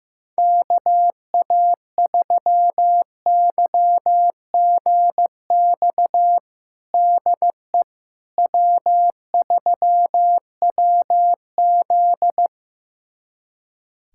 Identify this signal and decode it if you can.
Morse code, heard as a keyed tone, about 15 wpm, 700 Hz. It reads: KA3YGX DE W3WZ
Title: USA Callsign DE Callsign 15wpm